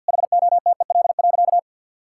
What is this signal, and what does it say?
Morse code, heard as a keyed tone, about 50 wpm, 700 Hz. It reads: HOTEL.
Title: Sentences from Top 1600 Words 50wpm